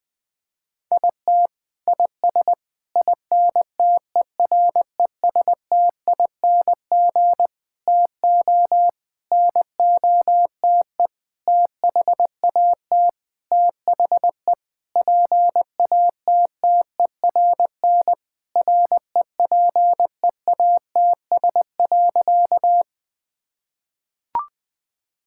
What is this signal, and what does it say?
Morse code, heard as a keyed tone, about 20 wpm, 700 Hz. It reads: IT IS INTERESTING TO NOTE THAT THE PATTERN REPEATS.
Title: Sentences with 6-Word Phrases - 20wpm